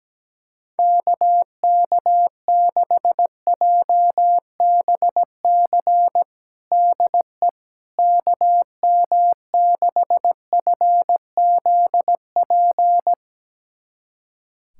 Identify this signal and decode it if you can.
Morse code, heard as a keyed tone, about 17 wpm, 700 Hz. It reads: KK6JBC DE KM6FZP
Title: USA Callsign DE Callsign - 6 Characters - 17wpm